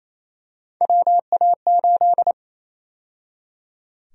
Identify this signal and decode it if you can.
Morse code, heard as a keyed tone, about 27 wpm, 700 Hz. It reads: WA8
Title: USA Call Sign Prefixes - 28wpm